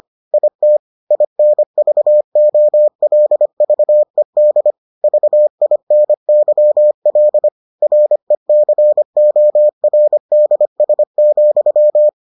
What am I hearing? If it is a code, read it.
Morse code, heard as a keyed tone, about 25 wpm, 600 Hz. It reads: IT INVOLVED VINYL RECORDS,